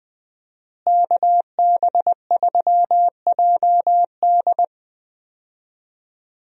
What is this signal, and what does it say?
Morse code, heard as a keyed tone, about 21 wpm, 700 Hz. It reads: KB3JD
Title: USA Call Signs - 5 Characters - Repeated 3x - 20wpm